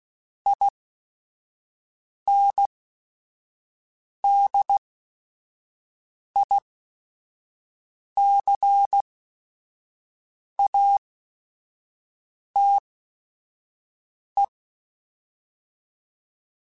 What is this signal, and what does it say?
Morse code, heard as a keyed tone, about 16 wpm, spaced as for 5 wpm, 800 Hz. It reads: INDICATE
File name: Technical Subjects